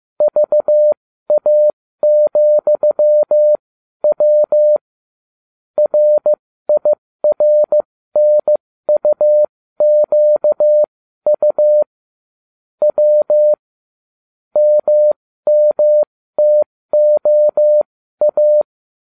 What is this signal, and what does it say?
Morse code, heard as a keyed tone, about 15 wpm, spaced as for 13 wpm, 600 Hz. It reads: VA,W RIRNUQU W MMTOA